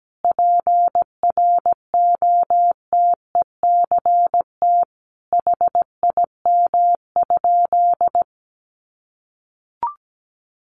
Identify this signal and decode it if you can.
Morse code, heard as a keyed tone, about 17 wpm, 700 Hz. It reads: PROTECT HIM?
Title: Sentences from Top 900 Words 17wpm